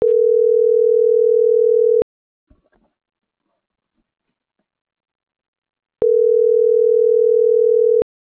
Ringback tone (North America), 440 + 480 Hz.